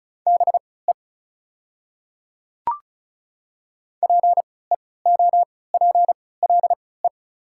Morse code, heard as a keyed tone, about 36 wpm, spaced as for 20 wpm, 690 Hz. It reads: BE PEOPLE
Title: 3x Character Spacing - Top 100 Words - 35wpm